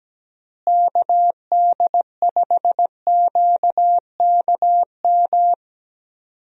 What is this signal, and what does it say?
Morse code, heard as a keyed tone, about 17 wpm, 700 Hz. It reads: KD5QKM
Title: USA Call Signs - 6 Characters - Repeated 3x - 17wpm